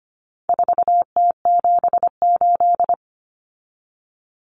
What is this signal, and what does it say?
Morse code, heard as a keyed tone, about 25 wpm, 700 Hz. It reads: 4T78
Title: Common Cut Numbers - 4-Digit 25wpm